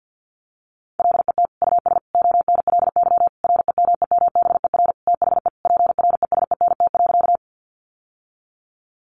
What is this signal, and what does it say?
Morse code, heard as a keyed tone, about 50 wpm, 700 Hz. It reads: LET US ONLY REMEMBER THE PRESENT.
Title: Sentences from Top 800 Words 50wpm